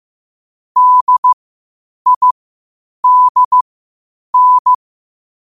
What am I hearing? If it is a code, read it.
Morse code, heard as a keyed tone, about 15 wpm, spaced as for 8 wpm, 1000 Hz. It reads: DIDN
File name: qotdfiles / 8wpm